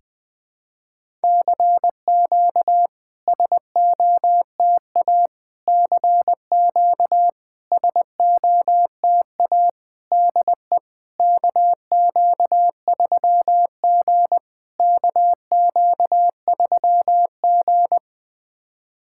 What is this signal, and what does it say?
Morse code, heard as a keyed tone, about 21 wpm, 700 Hz. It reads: CQ SOTA CQ SOTA DE KQ3G KQ3G
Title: SOTA QSO Practice 20wpm